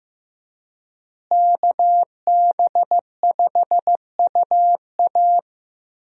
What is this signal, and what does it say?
Morse code, heard as a keyed tone, about 15 wpm, 700 Hz. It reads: KB5UA
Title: USA Call Signs - 5 Characters - Repeated 3x - 15wpm